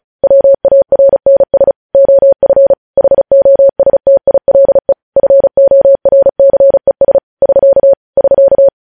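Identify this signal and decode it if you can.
Morse code, heard as a keyed tone, about 36 wpm, 550 Hz. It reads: WARNS OF HOSTILE FORCES <SK> <SK>
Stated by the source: Morse code practice sending news headlines at 35 wpm.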